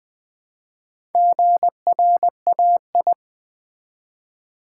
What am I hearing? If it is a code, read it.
Morse code, heard as a keyed tone, about 20 wpm, 700 Hz. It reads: GRAI
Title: Build A Word - 1 Letter At A Time - 5-Letter Words - 20wpm